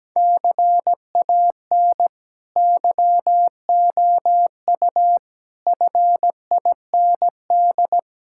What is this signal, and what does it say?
Morse code, heard as a keyed tone, about 17 wpm, 700 Hz. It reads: CAN YOU FIND
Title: Sentences from Top 1600 Words 17wpm